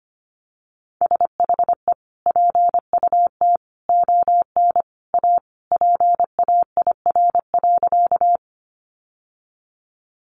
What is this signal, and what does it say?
Morse code, heard as a keyed tone, about 25 wpm, 700 Hz. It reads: SHE PUT ON A PAIR.
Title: Sentences from Top 1200 Words 25wpm